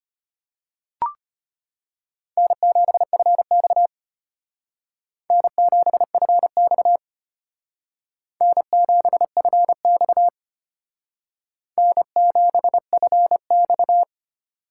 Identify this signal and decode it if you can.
Morse code, heard as a keyed tone, about 32 wpm, 700 Hz. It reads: N7FX N7FX N7FX N7FX